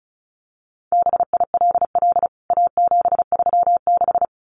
Morse code, heard as a keyed tone, about 34 wpm, 700 Hz. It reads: BILL A736
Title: CWT Exchange 35wpm